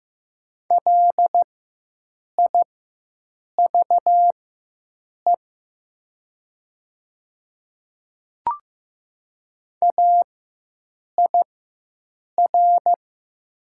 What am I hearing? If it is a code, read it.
Morse code, heard as a keyed tone, about 15 wpm, spaced as for 7 wpm, 700 Hz. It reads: LIVE AIR
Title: 4x Character Spacing - Top 300 Words - 15wpm